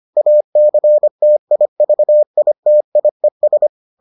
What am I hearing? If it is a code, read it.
Morse code, heard as a keyed tone, about 25 wpm, 600 Hz. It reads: ACTIVITIES